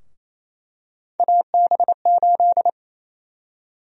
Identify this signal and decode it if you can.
Morse code, heard as a keyed tone, about 28 wpm, 700 Hz. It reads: AB8
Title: USA Call Sign Prefixes - Repeated 2x - 28wpm